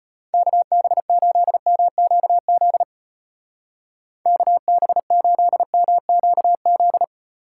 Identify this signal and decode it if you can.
Morse code, heard as a keyed tone, about 31 wpm, 700 Hz. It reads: KB8MQZ KB8MQZ
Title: Speed Racing - USA 2x3 Call Signs 25wpm